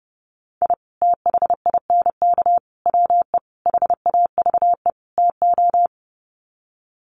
Morse code, heard as a keyed tone, about 28 wpm, 700 Hz. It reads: I THINK WE HAVE TO
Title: Sets of 5 Words 30wpm